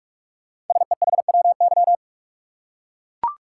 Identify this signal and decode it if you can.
Morse code, heard as a keyed tone, about 44 wpm, 700 Hz. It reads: SEHWY
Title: Code Groups - 5 Letters - 45wpm